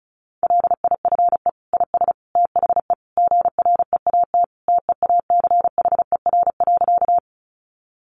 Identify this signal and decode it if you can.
Morse code, heard as a keyed tone, about 34 wpm, 700 Hz. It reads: LIFE IS THE GREAT TEACHER.
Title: Sentences from Top 900 Words 35wpm